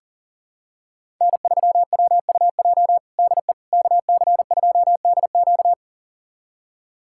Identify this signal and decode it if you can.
Morse code, heard as a keyed tone, about 40 wpm, 700 Hz. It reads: N3WUJ DE KC2DQ
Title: USA Callsign DE Callsign - 5 Characters - 40wpm